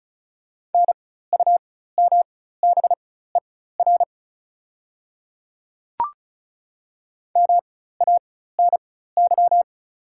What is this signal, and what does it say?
Morse code, heard as a keyed tone, about 36 wpm, spaced as for 16 wpm, 700 Hz. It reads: NUMBER MANY
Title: 4x Character Spacing - Top 100 Words - 35wpm